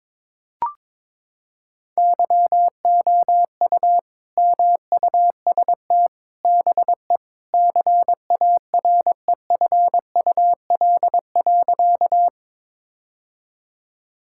Morse code, heard as a keyed tone, about 21 wpm, 700 Hz. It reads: YOU MUST BE CAREFUL.
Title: Sentences from Top 1600 Words 22wpm